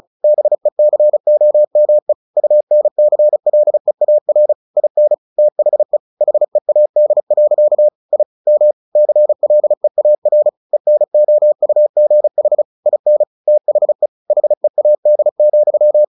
Morse code, heard as a keyed tone, about 34 wpm, 600 Hz. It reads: BECOME UNCLEAR IN THE HEAD. I M CLEAR ENOUGH IN THE HEAD,
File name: the old man and the sea 2 35wpm